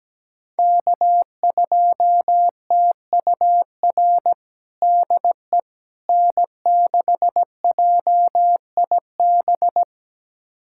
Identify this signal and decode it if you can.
Morse code, heard as a keyed tone, about 17 wpm, 700 Hz. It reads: K2TUR DE N6JIB
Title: USA Callsign DE Callsign - 5 Characters - 17wpm